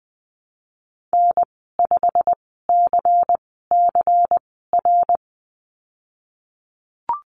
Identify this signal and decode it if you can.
Morse code, heard as a keyed tone, about 20 wpm, spaced as for 15 wpm, 700 Hz. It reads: N5CCR
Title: 186 Lesson 30g - Letter b - Call Signs 20wpm